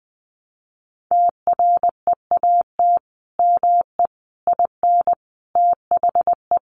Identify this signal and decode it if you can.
Morse code, heard as a keyed tone, about 20 wpm, 700 Hz. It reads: TREAT ME IN THE
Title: Sentences from Top 1300 Words 20wpm